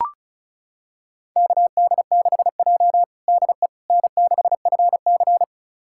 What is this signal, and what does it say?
Morse code, heard as a keyed tone, about 36 wpm, 700 Hz. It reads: KD6J DE N6FC
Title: USA Callsign DE Callsign - 4 Characters - 35wpm